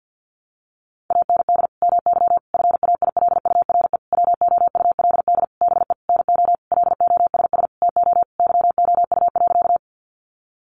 Morse code, heard as a keyed tone, about 50 wpm, 700 Hz. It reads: AND MY FAILURE WOULD BE NO LOSS TO YOU.
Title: Sentences from Top 1400 Words 50wpm